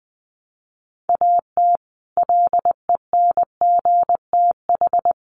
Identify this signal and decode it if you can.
Morse code, heard as a keyed tone, about 20 wpm, 700 Hz. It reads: AT LENGTH
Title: Sentences from Top 1000 Words 20wpm